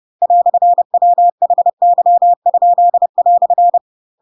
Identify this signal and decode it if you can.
Morse code, heard as a keyed tone, about 30 wpm, 700 Hz. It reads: "WHY?"
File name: Pinocchio / Pinocchio ch4 30 30 ik1hge